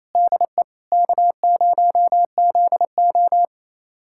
Morse code, heard as a keyed tone, about 28 wpm, 700 Hz. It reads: DE K0ZO
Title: USA Callsign DE Callsign - 4 Characters - 28wpm